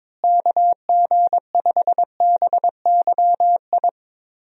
Morse code, heard as a keyed tone, about 22 wpm, 700 Hz. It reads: KG5BYI
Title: USA Call Signs - 6 Characters - Repeated 3x - 22wpm